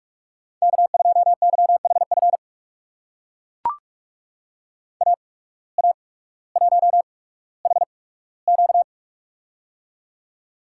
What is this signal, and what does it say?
Morse code, heard as a keyed tone, about 44 wpm, spaced as for 19 wpm, 700 Hz. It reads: K2YHF AA1HQ
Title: USA Call Signs - 5 Characters - 8x Character Spacing - 45wpm